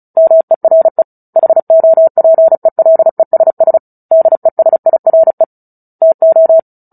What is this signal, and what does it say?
Morse code, heard as a keyed tone, about 34 wpm, 650 Hz. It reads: MERE HOPELESS DESIRE TO
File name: Book Two / 09-Chapter 9 - 35 WPM